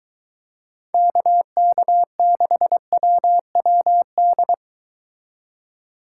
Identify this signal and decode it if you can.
Morse code, heard as a keyed tone, about 23 wpm, 700 Hz. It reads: KK6WWD